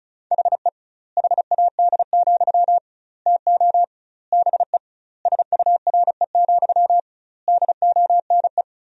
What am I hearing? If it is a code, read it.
Morse code, heard as a keyed tone, about 36 wpm, 700 Hz. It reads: HE HAD, TO BE SURE, DONE
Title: 2x Word-Spacing - Sentences from Top 500 Words - 35wpm